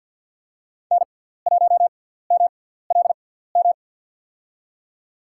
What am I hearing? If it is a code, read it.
Morse code, heard as a keyed tone, about 47 wpm, spaced as for 17 wpm, 700 Hz. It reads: N1MLM
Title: USA Call Signs - 5 Characters - 6x Character Spacing - 50wpm